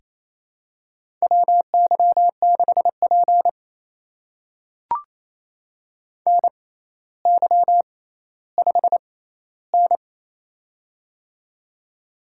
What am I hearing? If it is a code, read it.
Morse code, heard as a keyed tone, about 29 wpm, spaced as for 15 wpm, 700 Hz. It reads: WY6P NY5N
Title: USA Call Signs - 4 Characters - 6x Character Spacing - 28wpm